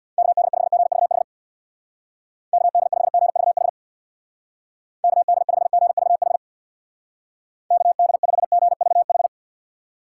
Morse code, heard as a keyed tone, about 65 wpm, 700 Hz. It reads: KD5GVH KD5GVH KD5GVH KD5GVH